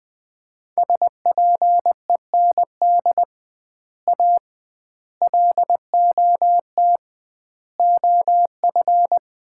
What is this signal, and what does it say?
Morse code, heard as a keyed tone, about 21 wpm, 700 Hz. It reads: SPEND A LOT OF
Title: 2x Word-Spacing - Sets of 5 Words - 20wpm